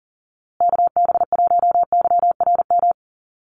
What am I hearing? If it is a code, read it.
Morse code, heard as a keyed tone, about 38 wpm, 700 Hz. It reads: KB1YRM
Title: USA Call Signs Repeated 2x 40wpm